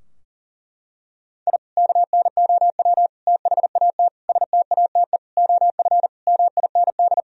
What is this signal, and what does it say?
Morse code, heard as a keyed tone, about 37 wpm, 700 Hz. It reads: I KNOW THAT STATE OF MIND
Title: Sentences from Top 1000 Words - Review Part 2 - 40wpm